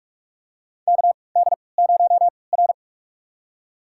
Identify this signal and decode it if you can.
Morse code, heard as a keyed tone, about 46 wpm, spaced as for 25 wpm, 700 Hz. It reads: KD0R
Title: USA Call Signs - 4 Characters - 3x Character Spacing - 45wpm